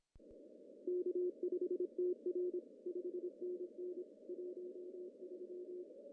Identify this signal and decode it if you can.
Morse code, heard as a keyed tone, about 25 wpm, 360 Hz. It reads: K5TR 5NN 13
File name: CW / cqww-cw-k5tr